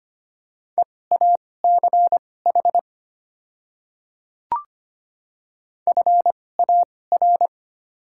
Morse code, heard as a keyed tone, about 25 wpm, spaced as for 18 wpm, 690 Hz. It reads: EACH FAR
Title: 2x Character Spacing - Top 300 Words - 25wpm